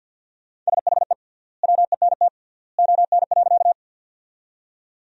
Morse code, heard as a keyed tone, about 51 wpm, 700 Hz. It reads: SHE WENT ON.
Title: 3x Word-Spacing - Sentences from Top 200 Words - 50wpm